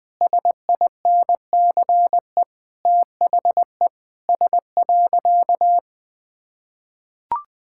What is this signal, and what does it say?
Morse code, heard as a keyed tone, about 20 wpm, 700 Hz. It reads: SINCE THE S.
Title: Sentences with 6-Word Phrases - 20wpm